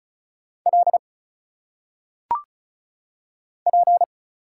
Morse code, heard as a keyed tone, about 35 wpm, 700 Hz. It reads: L P
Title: Single Letters - Phonetic Pronunciation 35wpm